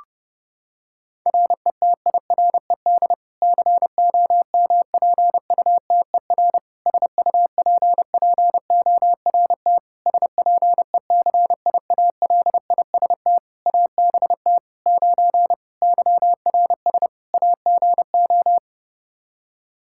Morse code, heard as a keyed tone, about 32 wpm, 700 Hz. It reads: RETIRED COMPUTER SUPPORT SPECIALIST ABT 9 YRS AGO
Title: Practice QSO - Exchange 3 - More Info - 30wpm